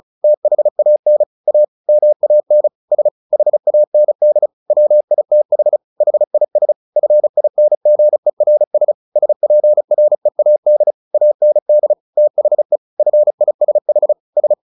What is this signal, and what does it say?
Morse code, heard as a keyed tone, about 34 wpm, 600 Hz. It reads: THAN A MAN S HAND WITH HIS FINGERS SPREAD AND THE FISH S
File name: the old man and the sea 2 35wpm